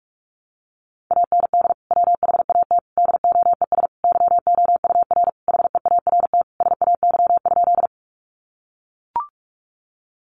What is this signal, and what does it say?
Morse code, heard as a keyed tone, about 44 wpm, 700 Hz. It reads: AND WHAT DOES YOUR HEART SAY?
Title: Sentences from Top 1200 Words 45wpm